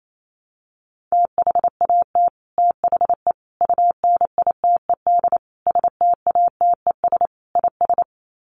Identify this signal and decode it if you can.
Morse code, heard as a keyed tone, about 28 wpm, 700 Hz. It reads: THAT THE UNITED STATES IS
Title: Sets of 5 Words Encore 28wpm